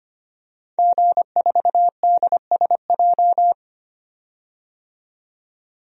Morse code, heard as a keyed tone, about 25 wpm, 700 Hz. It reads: G4DSJ
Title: DXCC-100 Call Signs - 5 Characters - 25wpm